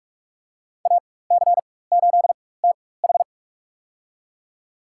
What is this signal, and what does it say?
Morse code, heard as a keyed tone, about 43 wpm, spaced as for 21 wpm, 700 Hz. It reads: AC8TH
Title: USA Call Signs - 5 Characters - 4x Character Spacing - 45wpm